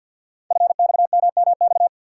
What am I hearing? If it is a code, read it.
Morse code, heard as a keyed tone, about 48 wpm, 700 Hz. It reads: FXMMX
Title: Code Groups - 5 Letters - 50wpm